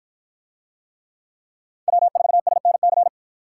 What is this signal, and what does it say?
Morse code, heard as a keyed tone, about 53 wpm, 700 Hz. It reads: W4SNC